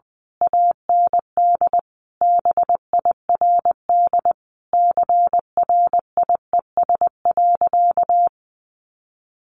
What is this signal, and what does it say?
Morse code, heard as a keyed tone, about 20 wpm, 700 Hz. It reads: AND BIRD CRIES.
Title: A Sound Of Thunder 20wpm